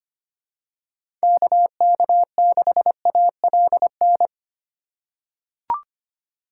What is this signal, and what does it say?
Morse code, heard as a keyed tone, about 25 wpm, 700 Hz. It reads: KK6ALN
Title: USA Call Signs - 6 Characters - 3x Character Spacing - 25wpm